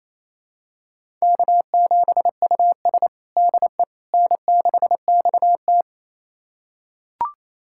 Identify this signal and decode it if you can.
Morse code, heard as a keyed tone, about 27 wpm, 700 Hz. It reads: K7US DE N6XT
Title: USA Callsign DE Callsign - 4 Characters - 28wpm